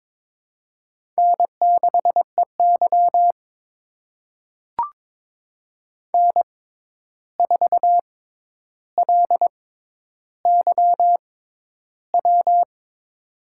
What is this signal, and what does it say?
Morse code, heard as a keyed tone, about 22 wpm, spaced as for 11 wpm, 700 Hz. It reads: N6EY N4LYW
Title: USA Call Signs - 6x Character Spacing - 22wpm